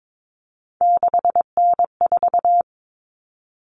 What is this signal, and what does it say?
Morse code, heard as a keyed tone, about 22 wpm, 700 Hz. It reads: 6N4
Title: Common Cut Numbers - 3-Digit 22wpm